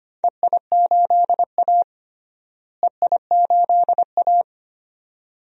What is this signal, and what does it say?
Morse code, heard as a keyed tone, about 25 wpm, 700 Hz. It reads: EI8A EI8A
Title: DXCC-100 Call Signs - 4 Characters - Repeated 2x - 25wpm